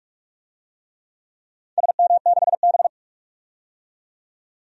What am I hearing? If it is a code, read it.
Morse code, heard as a keyed tone, about 41 wpm, 700 Hz. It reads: SM6B
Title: DXCC-100 Call Signs - 4 Characters - 45wpm